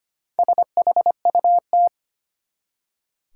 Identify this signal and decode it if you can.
Morse code, heard as a keyed tone, about 25 wpm, 700 Hz. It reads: SHUT
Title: Four Letter Words - 25wpm